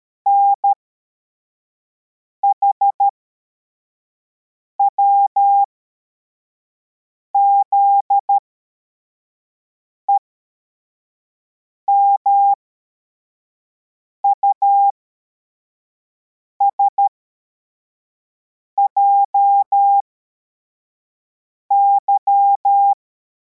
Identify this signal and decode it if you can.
Morse code, heard as a keyed tone, about 13 wpm, spaced as for 4 wpm, 800 Hz. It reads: NHWZE MUSJY
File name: formation / cw / Text85